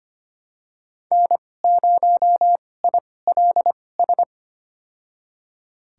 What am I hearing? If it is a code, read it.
Morse code, heard as a keyed tone, about 25 wpm, spaced as for 18 wpm, 700 Hz. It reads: N0ILS
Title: USA Call Signs - 5 Characters - 2x Character Spacing - 25wpm